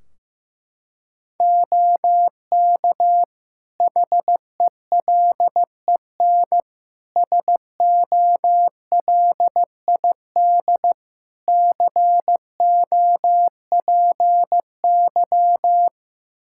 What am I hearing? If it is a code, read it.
Morse code, heard as a keyed tone, about 15 wpm, 700 Hz. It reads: OK HELEN SOLID COPY
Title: Practice QSO - Exchange 3 - More Info - 15wpm